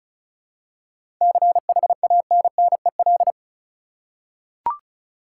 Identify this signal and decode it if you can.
Morse code, heard as a keyed tone, about 36 wpm, 700 Hz. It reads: CHANNEL
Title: Seven Letter Words - 35wpm